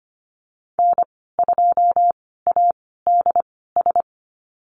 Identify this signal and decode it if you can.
Morse code, heard as a keyed tone, about 25 wpm, spaced as for 16 wpm, 700 Hz. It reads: N2ADS
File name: Lesson 15g - Number 5 - Call Signs 25wpm